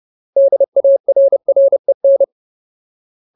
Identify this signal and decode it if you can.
Morse code, heard as a keyed tone, about 30 wpm, 550 Hz. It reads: DARREN
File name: Ultimate Word List / 30WPM Ultimate